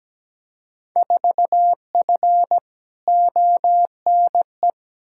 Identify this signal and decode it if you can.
Morse code, heard as a keyed tone, about 17 wpm, 700 Hz. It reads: 4F ONE
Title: Field Day Exchanges Encore - 17wpm